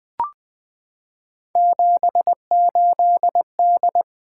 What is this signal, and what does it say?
Morse code, heard as a keyed tone, about 20 wpm, 700 Hz. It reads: 78D
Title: Three Letter-Number Combinations - 20wpm